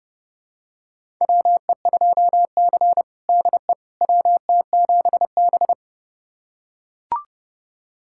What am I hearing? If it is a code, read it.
Morse code, heard as a keyed tone, about 30 wpm, 700 Hz. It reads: WE2C DE WT7B
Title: USA Callsign DE Callsign - 4 Characters - 30wpm